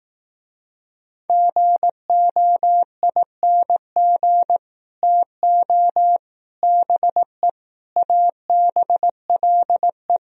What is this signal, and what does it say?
Morse code, heard as a keyed tone, about 18 wpm, 700 Hz. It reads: GOING TO BE ABLE
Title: Speed-Racing - Sets of 5 Words 15wpm